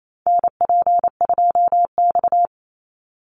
Morse code, heard as a keyed tone, about 28 wpm, 700 Hz. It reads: NP2X
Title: Call Signs - 28wpm